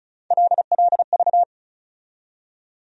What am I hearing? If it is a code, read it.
Morse code, heard as a keyed tone, about 36 wpm, 700 Hz. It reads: LLV
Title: USA Call Sign Postfixes - 3 Letters 35wpm